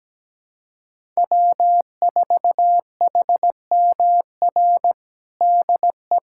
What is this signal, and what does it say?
Morse code, heard as a keyed tone, about 17 wpm, 700 Hz. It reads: W4HMR DE
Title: USA Callsign DE Callsign - 5 Characters - 17wpm